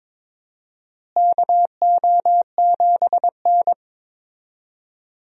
Morse code, heard as a keyed tone, about 22 wpm, 700 Hz. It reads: KO7N
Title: USA Call Signs - 4 Characters - Repeated 3x - 22wpm